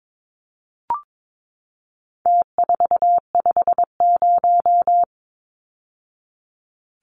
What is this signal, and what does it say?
Morse code, heard as a keyed tone, about 22 wpm, 700 Hz. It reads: T450
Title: Numbers - Contest Style - 4-Digit 22wpm